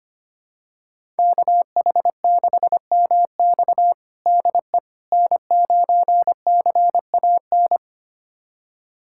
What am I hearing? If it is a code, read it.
Morse code, heard as a keyed tone, about 25 wpm, 700 Hz. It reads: KH6MX DE N9CAN
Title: USA Callsign DE Callsign - 5 Characters - 25wpm